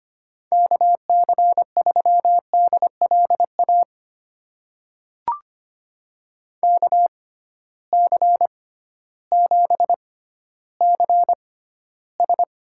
Morse code, heard as a keyed tone, about 25 wpm, spaced as for 13 wpm, 700 Hz. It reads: KC3DLA KC7CS
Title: USA Call Signs - 6 Characters - 6x Character Spacing - 25wpm